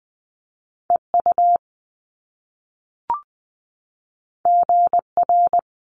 Morse code, heard as a keyed tone, about 20 wpm, 700 Hz. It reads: EU GR
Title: Top Two Letter Combinations in Words 20wpm